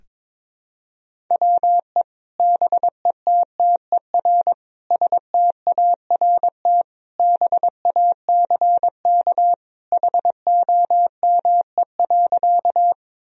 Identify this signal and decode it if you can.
Morse code, heard as a keyed tone, about 22 wpm, 700 Hz. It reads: WE BETTER START BACK HOME.
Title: Sentences from Top 1000 Words - Review Part 1 - 22wpm